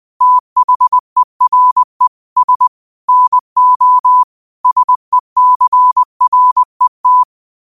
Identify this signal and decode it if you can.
Morse code, heard as a keyed tone, about 20 wpm, 1000 Hz. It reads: THERE S NO SECRET
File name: qotdfiles / 20wpm